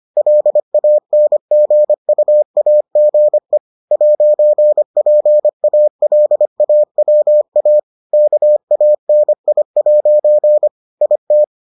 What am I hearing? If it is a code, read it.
Morse code, heard as a keyed tone, about 25 wpm, 600 Hz. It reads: LANGUAGE 'PALAWA KANI' IT